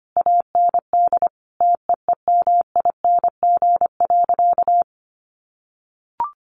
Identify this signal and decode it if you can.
Morse code, heard as a keyed tone, about 25 wpm, 700 Hz. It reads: AND TEEMING.
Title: A Sound Of Thunder 25wpm